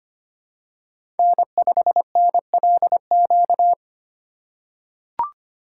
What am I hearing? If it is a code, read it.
Morse code, heard as a keyed tone, about 25 wpm, 700 Hz. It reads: N5NLQ
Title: USA Call Signs - 5 Characters - 4x Character Spacing - 25wpm